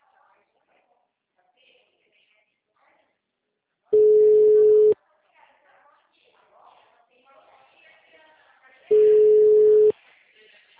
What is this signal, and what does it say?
Ringback tone (Europe, 425 Hz), with speech.
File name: out- -2028- - - .17908